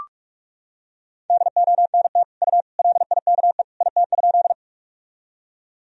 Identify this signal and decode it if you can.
Morse code, heard as a keyed tone, about 46 wpm, 700 Hz. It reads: DONT U LIKE IT?
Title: Sentences from Top 300 Words and Abbreviations 45wpm